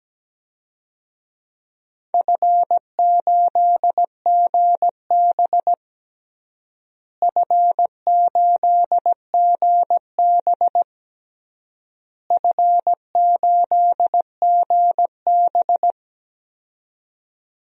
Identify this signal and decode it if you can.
Morse code, heard as a keyed tone, about 17 wpm, 700 Hz. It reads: F8GB F8GB F8GB
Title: DXCC-100 Call Signs - 4 Characters - Repeated 3x - 17wpm